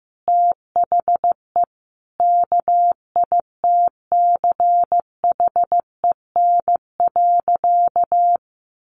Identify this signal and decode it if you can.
Morse code, heard as a keyed tone, about 15 wpm, 700 Hz. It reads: THE KITCHEN.
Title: Easy Phrases 15wpm